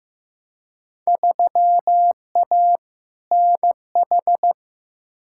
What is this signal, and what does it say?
Morse code, heard as a keyed tone, about 16 wpm, 700 Hz. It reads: 3A NH
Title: Field Day Exchanges Encore - 15wpm